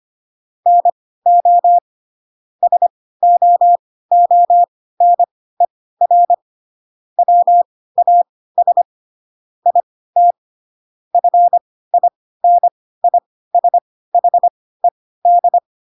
Morse code, heard as a keyed tone, about 25 wpm, spaced as for 16 wpm, 700 Hz. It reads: NO SOONER WAS IT FINISHED
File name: Pinocchio / Pinocchio ch3 10 25 ik1hge